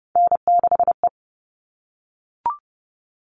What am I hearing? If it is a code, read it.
Morse code, heard as a keyed tone, about 30 wpm, 700 Hz. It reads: N6E
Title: All Cut Numbers - 3-Digit 30wpm